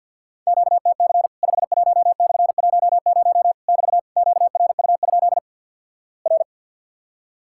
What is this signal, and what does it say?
Morse code, heard as a keyed tone, about 50 wpm, 700 Hz. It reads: QTC 51/10 <BT> QRU? R
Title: Worked All Europe DX Contest - QTC Exchange - 50wpm